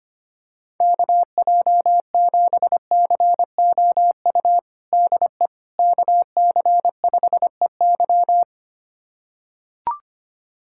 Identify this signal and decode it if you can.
Morse code, heard as a keyed tone, about 25 wpm, 700 Hz. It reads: KJ7COU DE KC5EY
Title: USA Callsign DE Callsign 25wpm